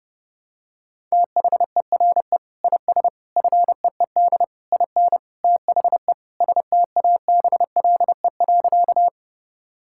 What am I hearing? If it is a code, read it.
Morse code, heard as a keyed tone, about 30 wpm, 700 Hz. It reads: THERE IS FEED IN THE STABLE.
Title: Sentences from Top 1600 Words 30wpm